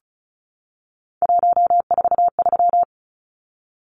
Morse code, heard as a keyed tone, about 34 wpm, 700 Hz. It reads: 143
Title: Numbers - 3-Digit 35wpm